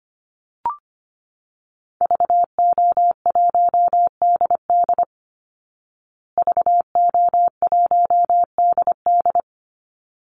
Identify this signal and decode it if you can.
Morse code, heard as a keyed tone, about 25 wpm, 700 Hz. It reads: VO1DD VO1DD
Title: Call Signs Repeated 3x - 25wpm